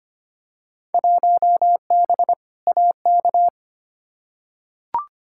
Morse code, heard as a keyed tone, about 25 wpm, 700 Hz. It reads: 1B AK
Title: Field Day Exchanges Encore - 25wpm